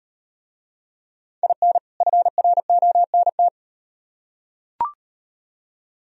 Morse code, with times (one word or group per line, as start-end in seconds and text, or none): 1.4-1.8 IN
2.0-3.5 FRONT